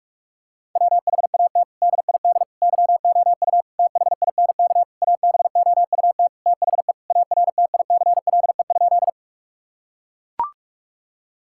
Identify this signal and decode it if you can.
Morse code, heard as a keyed tone, about 42 wpm, 700 Hz. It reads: WHAT DID YOU THINK ABOUT THE ARTICLE?
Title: Sentences from Top 1000 Words - Review Part 5 - 45wpm